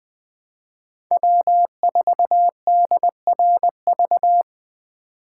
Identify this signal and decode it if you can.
Morse code, heard as a keyed tone, about 23 wpm, 700 Hz. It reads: W4DRV
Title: Speed Racing - USA 1x3 Call Signs 20wpm